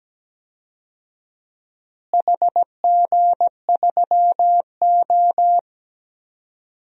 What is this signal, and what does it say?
Morse code, heard as a keyed tone, about 17 wpm, 700 Hz. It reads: HG3O
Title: DXCC-100 Call Signs - 4 Characters - Repeated 3x - 17wpm